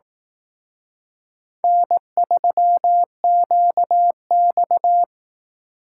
Morse code, heard as a keyed tone, about 18 wpm, 700 Hz. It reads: N3QX
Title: Speed Racing - USA Call Signs 15wpm